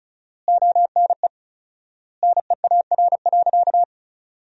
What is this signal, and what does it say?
Morse code, heard as a keyed tone, about 34 wpm, 700 Hz. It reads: ONE NEAR.
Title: 4x Word-Spacing - Sentences from Top 200 Words - 35wpm